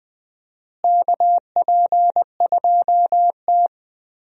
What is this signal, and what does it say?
Morse code, heard as a keyed tone, about 21 wpm, 700 Hz. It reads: KP2T
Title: DXCC-100 Call Signs - 4 Characters - Repeated 3x - 20wpm